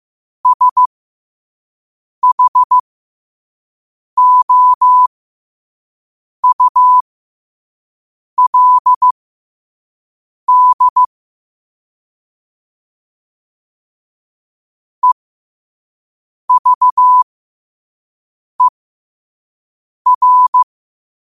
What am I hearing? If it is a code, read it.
Morse code, heard as a keyed tone, about 15 wpm, spaced as for 5 wpm, 1000 Hz. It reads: SHOULD EVER
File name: 5wpm